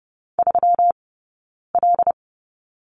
Morse code, heard as a keyed tone, about 30 wpm, spaced as for 9 wpm, 700 Hz. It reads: 3L
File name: Lesson 20d - Number 3 - Two Characters 30wpm